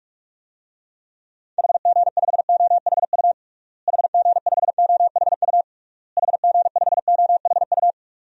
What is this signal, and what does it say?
Morse code, heard as a keyed tone, about 46 wpm, 700 Hz. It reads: HG5OHU HG5OHU HG5OHU
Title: DXCC-100 Call Signs - 6 Characters - Repeated 3x - 45wpm